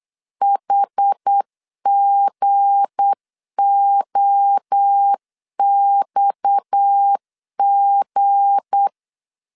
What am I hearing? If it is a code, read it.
Morse code, heard as a keyed tone, about 8 wpm, 800 Hz. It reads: HGOXG
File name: LEZIONI / 7 GF9Z / 7 G 8wpm 3118